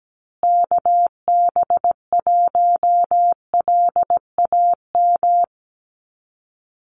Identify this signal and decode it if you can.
Morse code, heard as a keyed tone, about 17 wpm, 700 Hz. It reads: KB1LAM
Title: USA Call Signs Repeated 3x 17wpm